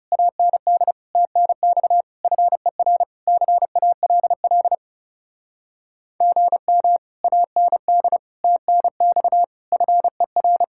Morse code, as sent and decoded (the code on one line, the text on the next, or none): .- -. -.. / - -. -..- / ..-. .-. / -.-. .- .-.. .-.. / --. -- / .- -. -.. / - -. -..- / ..-. .-.
AND TNX FER CALL GM AND TNX FER